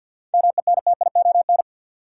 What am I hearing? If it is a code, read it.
Morse code, heard as a keyed tone, about 50 wpm, 700 Hz. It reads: MENTION
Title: Seven Letter Words - 50wpm